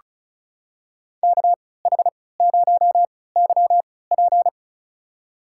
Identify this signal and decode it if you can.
Morse code, heard as a keyed tone, about 36 wpm, spaced as for 20 wpm, 700 Hz. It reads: KH0YP
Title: USA Call Signs - 3x Character Spacing - 35wpm